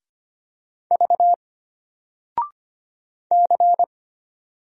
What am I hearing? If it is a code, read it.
Morse code, heard as a keyed tone, about 25 wpm, 700 Hz. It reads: V C
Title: Single Letter-Number 25wpm